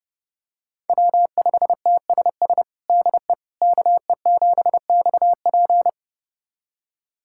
Morse code, heard as a keyed tone, about 30 wpm, 700 Hz. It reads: W5TSS DE KE7XP
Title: USA Callsign DE Callsign 30wpm